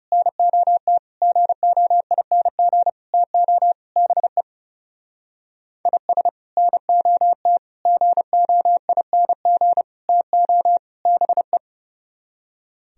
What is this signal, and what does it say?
Morse code, heard as a keyed tone, about 39 wpm, 700 Hz. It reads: NOT GOING TO BE IS NOT GOING TO BE